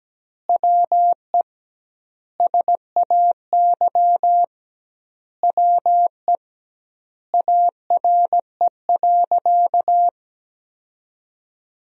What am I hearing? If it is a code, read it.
Morse code, heard as a keyed tone, about 17 wpm, 700 Hz. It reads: WE SAY WE ARE.
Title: 2x Word-Spacing - Sentences from Top 200 Words - 17wpm